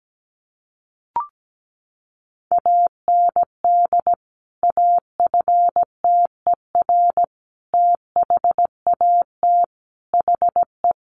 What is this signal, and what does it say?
Morse code, heard as a keyed tone, about 17 wpm, 700 Hz. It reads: AND AFTER THAT HE
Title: Sentences from Top 1400 Words 17wpm